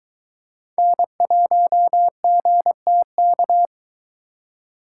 Morse code, heard as a keyed tone, about 22 wpm, 700 Hz. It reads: N1GTK